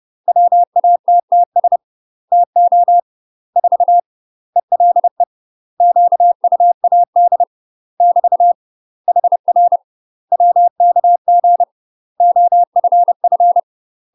Morse code, heard as a keyed tone, about 30 wpm, 700 Hz. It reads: WATTS TO 4 ELE QUAD <BT> HR WKG OFF
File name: UK MASTER QSOs TRAINING / 30WPM / UK MASTER QSOs 0023